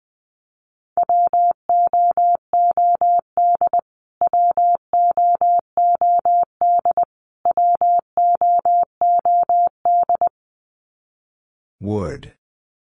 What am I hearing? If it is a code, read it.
Morse code, heard as a keyed tone, about 20 wpm, 700 Hz. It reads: WOOD WOOD WOOD
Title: Top 300 Words Repeated 20wpm